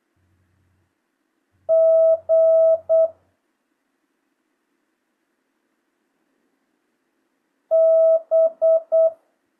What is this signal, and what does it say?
Morse code, heard as a keyed tone, about 8 wpm, 650 Hz. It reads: G B
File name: jh8cbh / 2025cw